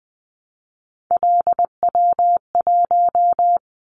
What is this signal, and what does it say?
Morse code, heard as a keyed tone, about 20 wpm, 700 Hz. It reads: LW1
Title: Callsign Prefixes for Top 100 DXCC - 20wpm